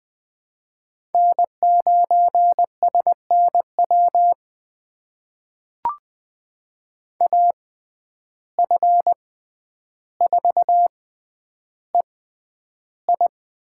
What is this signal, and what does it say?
Morse code, heard as a keyed tone, about 20 wpm, spaced as for 10 wpm, 700 Hz. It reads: N9SNW AF4EI
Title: USA Call Signs - 5 Characters - 6x Character Spacing - 20wpm